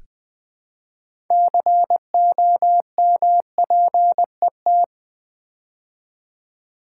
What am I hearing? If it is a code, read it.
Morse code, heard as a keyed tone, about 20 wpm, 700 Hz. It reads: COMPET
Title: Build A Word - 3 Letters At A Time - 9-Letter Words - 20wpm